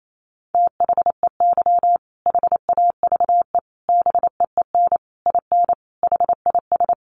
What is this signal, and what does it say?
Morse code, heard as a keyed tone, about 28 wpm, 700 Hz. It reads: THEY HAVE BEEN IN HIS
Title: Sentences from Top 700 Words 28wpm